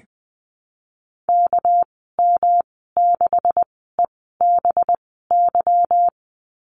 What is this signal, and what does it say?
Morse code, heard as a keyed tone, about 20 wpm, spaced as for 14 wpm, 700 Hz. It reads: KM6EBY
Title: 213 Lesson 34g - Number 8 - Call Signs 20wpm